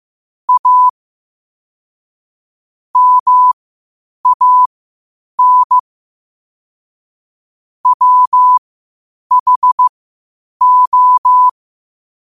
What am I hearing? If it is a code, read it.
Morse code, heard as a keyed tone, about 15 wpm, spaced as for 8 wpm, 1000 Hz. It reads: A MAN WHO
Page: qotd-at-8-wpm